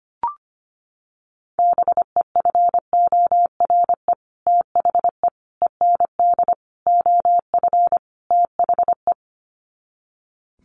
Morse code, heard as a keyed tone, about 25 wpm, 700 Hz. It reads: BEFORE THE END OF THE
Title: Sets of 5 Words Encore 25wpm